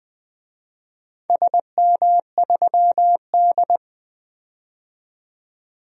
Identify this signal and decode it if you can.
Morse code, heard as a keyed tone, about 20 wpm, 700 Hz. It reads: SM3D
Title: DXCC-100 Call Signs - 4 Characters - 20wpm